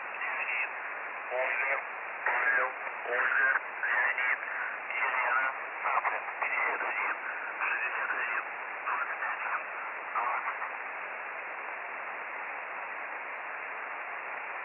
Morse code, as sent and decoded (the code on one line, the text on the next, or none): none